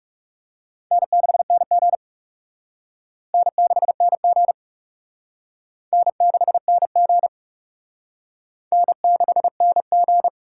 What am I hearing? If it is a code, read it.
Morse code, heard as a keyed tone, about 35 wpm, 700 Hz. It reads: N6NG N6NG N6NG N6NG
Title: Speed Racing - USA 1x2 Call Signs 30wpm